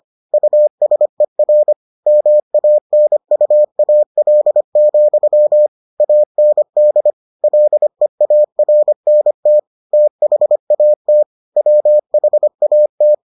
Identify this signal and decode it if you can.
Morse code, heard as a keyed tone, about 25 wpm, 600 Hz. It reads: USER MANUAL, AND LEARNT THAT WHAT